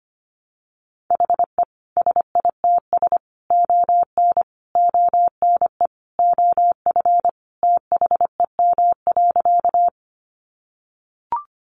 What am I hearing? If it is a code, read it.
Morse code, heard as a keyed tone, about 25 wpm, 700 Hz. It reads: HE SITS ON ONE OF THEM.
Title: Additional Easy Phrases 25wpm